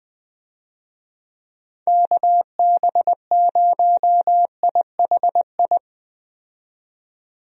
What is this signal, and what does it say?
Morse code, heard as a keyed tone, about 20 wpm, 700 Hz. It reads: KB0IHI
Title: USA Call Signs - 6 Characters - 20wpm